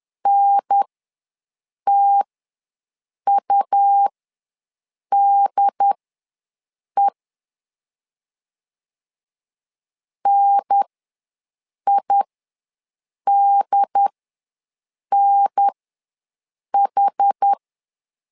Morse code, heard as a keyed tone, about 10 wpm, spaced as for 6 wpm, 800 Hz. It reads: NTUDE NIDNH
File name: 2 UHDN ws 10wpm